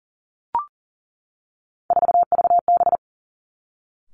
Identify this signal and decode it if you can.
Morse code, heard as a keyed tone, about 38 wpm, 700 Hz. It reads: 4VB
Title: Three Challenging Letter-Number Combinations 40wpm